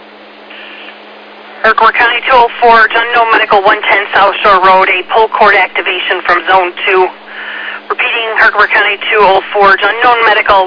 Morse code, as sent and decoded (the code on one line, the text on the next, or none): none